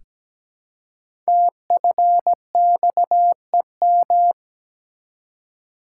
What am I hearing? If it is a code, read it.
Morse code, heard as a keyed tone, about 17 wpm, 700 Hz. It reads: TFXEM
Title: Code Groups - 5 Letters - 17wpm